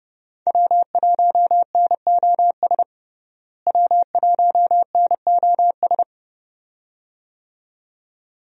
Morse code, heard as a keyed tone, about 30 wpm, 700 Hz. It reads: W1NOS W1NOS
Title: USA Call Signs - 5 Characters - Repeated 2x - 30wpm